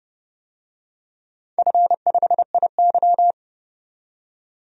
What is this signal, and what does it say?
Morse code, heard as a keyed tone, about 30 wpm, 700 Hz. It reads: F5IY
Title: DXCC-100 Call Signs - 4 Characters - 30wpm